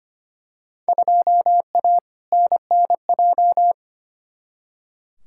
Morse code, heard as a keyed tone, about 25 wpm, 700 Hz. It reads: 2A NNJ
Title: Field Day Exchanges - 25wpm